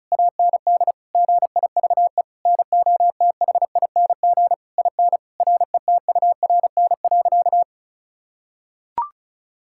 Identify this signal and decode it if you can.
Morse code, heard as a keyed tone, about 36 wpm, 700 Hz. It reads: AND GIVE NOTHING IN RETURN.
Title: Sentences from Top 1000 Words - Review Part 6 - 35wpm